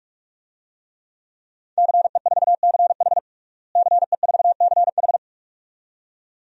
Morse code, heard as a keyed tone, about 46 wpm, 700 Hz. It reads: CE4CH CE4CH
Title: DXCC-100 Call Signs - 5 Characters - Repeated 2x - 45wpm